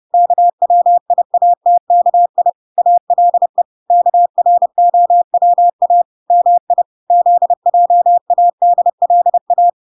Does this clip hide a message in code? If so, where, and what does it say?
Yes: Morse code from 0.1 to 9.7 s, heard as a keyed tone, about 30 wpm, 700 Hz. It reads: KWIATKI ALE KROWA MI ZJADLA